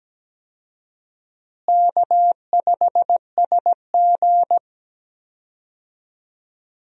Morse code, heard as a keyed tone, about 17 wpm, 700 Hz. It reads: K5SG
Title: USA Call Signs - 4 Characters - Repeated 3x - 17wpm